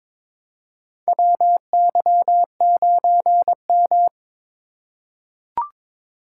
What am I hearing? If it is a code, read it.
Morse code, heard as a keyed tone, about 22 wpm, 700 Hz. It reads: WY9M
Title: USA Call Signs - 2x Character Spacing - 22wpm